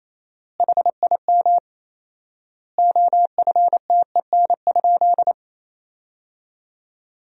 Morse code, heard as a keyed tone, about 27 wpm, 700 Hz. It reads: HIM OFTEN?
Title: 4x Word-Spacing - Sentences from Top 400 Words - 28wpm